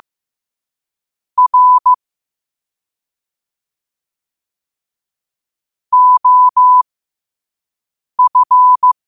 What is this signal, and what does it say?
Morse code, heard as a keyed tone, about 15 wpm, spaced as for 5 wpm, 1000 Hz. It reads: R OF
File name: qotdfiles / 5wpm